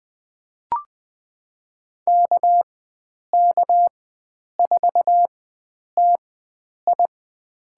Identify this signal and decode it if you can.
Morse code, heard as a keyed tone, about 20 wpm, spaced as for 9 wpm, 700 Hz. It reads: KK4TI
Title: USA Call Signs - 4x Character Spacing - 20wpm